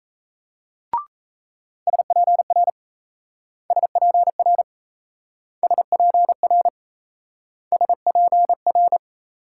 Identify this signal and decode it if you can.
Morse code, heard as a keyed tone, about 35 wpm, 700 Hz. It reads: SPR SPR SPR SPR